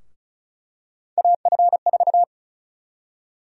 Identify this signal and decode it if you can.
Morse code, heard as a keyed tone, about 36 wpm, 700 Hz. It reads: AF4
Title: USA Call Sign Prefixes - 35wpm